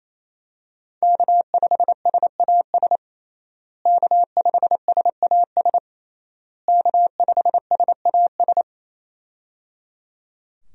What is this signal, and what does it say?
Morse code, heard as a keyed tone, about 28 wpm, 700 Hz. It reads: K5SAS K5SAS K5SAS
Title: USA Call Signs - 5 Characters - Repeated 3x - 28wpm